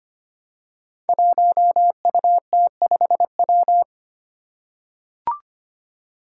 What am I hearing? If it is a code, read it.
Morse code, heard as a keyed tone, about 25 wpm, 700 Hz. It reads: 1UT5W
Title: Five Challenging Letter-Number Combinations 25wpm